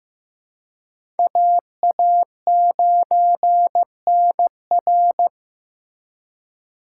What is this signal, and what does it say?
Morse code, heard as a keyed tone, about 15 wpm, 700 Hz. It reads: AA9NR
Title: USA Call Signs - 5 Characters - 2x Character Spacing - 15wpm